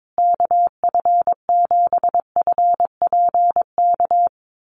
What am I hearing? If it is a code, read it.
Morse code, heard as a keyed tone, about 22 wpm, 700 Hz. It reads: KF7FPK
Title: USA Call Signs Repeated 2x 22wpm